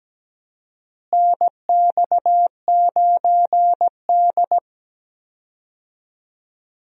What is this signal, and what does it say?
Morse code, heard as a keyed tone, about 17 wpm, 700 Hz. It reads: NX9D
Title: USA Call Signs - 4 Characters - Repeated 2x - 17wpm